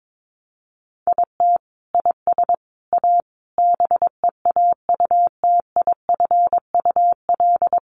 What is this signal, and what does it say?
Morse code, heard as a keyed tone, about 22 wpm, 700 Hz. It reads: IT IS A BEAUTIFUL
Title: Sentences from Top 1200 Words 22wpm